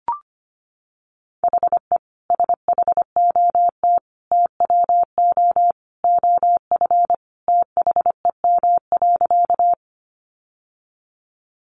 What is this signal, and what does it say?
Morse code, heard as a keyed tone, about 25 wpm, 700 Hz. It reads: HE SHOT TWO OF THEM.
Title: Sentences from Top 1100 Words 25wpm